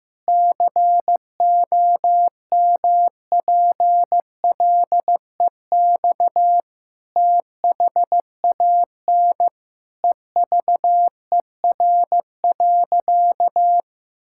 Morse code, heard as a keyed tone, about 15 wpm, 700 Hz. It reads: COMPLEX THAN EVER.
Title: Sentences from Top 1000 Words - Review Part 6 - 15wpm